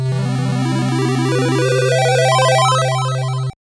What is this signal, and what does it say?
There is no code.